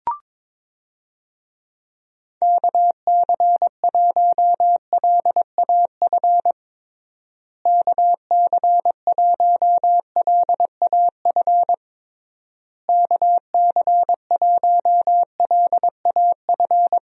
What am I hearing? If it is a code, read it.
Morse code, heard as a keyed tone, about 22 wpm, 700 Hz. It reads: KC1LAF KC1LAF KC1LAF
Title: USA Call Signs - 6 Characters - Repeated 3x - 22wpm